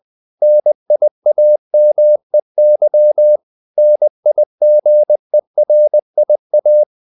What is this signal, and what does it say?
Morse code, heard as a keyed tone, about 20 wpm, 600 Hz. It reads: NIAMEY NIGERIA
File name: CW / Practice / NationalCapitals20